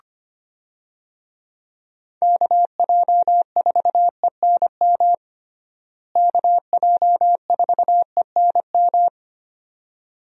Morse code, heard as a keyed tone, about 25 wpm, 700 Hz. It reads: KJ4ENM KJ4ENM
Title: USA Call Signs - 6 Characters - Repeated 2x - 25wpm